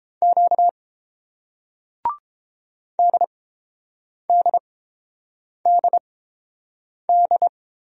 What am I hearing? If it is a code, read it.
Morse code, heard as a keyed tone, about 30 wpm, spaced as for 7 wpm, 700 Hz. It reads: Q DDDD